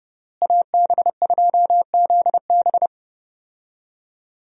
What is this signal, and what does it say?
Morse code, heard as a keyed tone, about 30 wpm, 700 Hz. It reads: AB2ZB
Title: USA Call Signs - 5 Characters - Repeated 3x - 30wpm